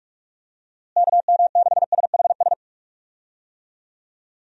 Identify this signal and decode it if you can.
Morse code, heard as a keyed tone, about 43 wpm, 700 Hz. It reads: KM6SHS
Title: USA Call Signs - 6 Characters - Repeated 2x - 45wpm